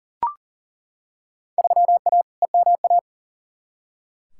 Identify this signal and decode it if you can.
Morse code, heard as a keyed tone, about 39 wpm, 700 Hz. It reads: 3A EMA
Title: Field Day Exchanges - 40wpm